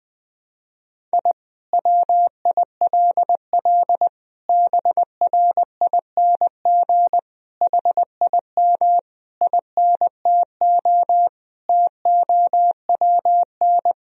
Morse code, heard as a keyed tone, about 20 wpm, 700 Hz. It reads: I WILL BRING HIM INTO TOWN
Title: Sentences from Top 1000 Words - Review Part 3 - 20wpm